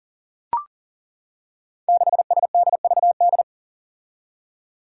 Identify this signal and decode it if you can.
Morse code, heard as a keyed tone, about 42 wpm, 690 Hz. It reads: 6SDVD
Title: Five Challenging Letter-Number Combinations 40wpm